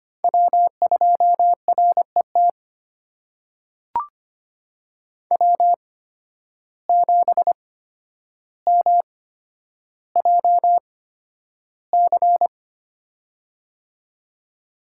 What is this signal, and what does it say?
Morse code, heard as a keyed tone, about 25 wpm, spaced as for 11 wpm, 700 Hz. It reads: W2RET W7MJC
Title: USA Call Signs - 5 Characters - 8x Character Spacing - 25wpm